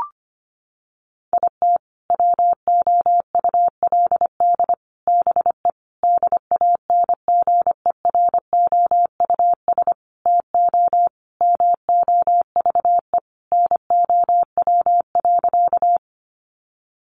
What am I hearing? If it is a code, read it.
Morse code, heard as a keyed tone, about 25 wpm, 700 Hz. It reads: IT WOULD BE DANGEROUS TO MOVE NOW.
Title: Sentences from Top 1200 Words 25wpm